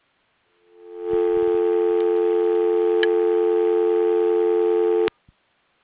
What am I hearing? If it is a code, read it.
Dial tone (North America), 350 + 440 Hz.